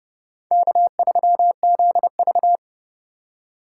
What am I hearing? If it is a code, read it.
Morse code, heard as a keyed tone, about 31 wpm, 700 Hz. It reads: K3ZV
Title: USA Call Signs - 4 Characters - 2x Character Spacing - 30wpm